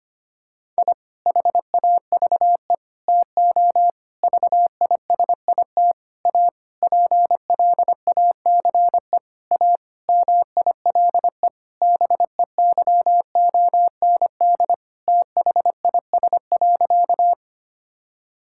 Morse code, heard as a keyed tone, about 25 wpm, 700 Hz. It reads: I HAVE TO VISIT A PLACE A MILE BEYOND THIS.
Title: Sentences from Top 1000 Words - Review Part 6 - 25wpm